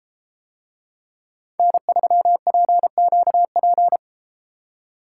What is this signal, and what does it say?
Morse code, heard as a keyed tone, about 33 wpm, 700 Hz. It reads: N3PQP